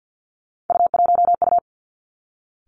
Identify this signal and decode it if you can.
Morse code, heard as a keyed tone, about 50 wpm, 700 Hz. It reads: U1U
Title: Three Challenging Letter-Number Combinations 50wpm